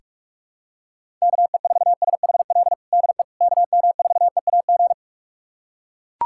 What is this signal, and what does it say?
Morse code, heard as a keyed tone, about 43 wpm, 700 Hz. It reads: KE4SHL DE KM4EAG
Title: USA Callsign DE Callsign - 6 Characters - 45wpm